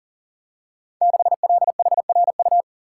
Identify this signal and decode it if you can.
Morse code, heard as a keyed tone, about 39 wpm, 700 Hz. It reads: 6LHRU
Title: Code Groups - 5 Letter:Numbers - 40wpm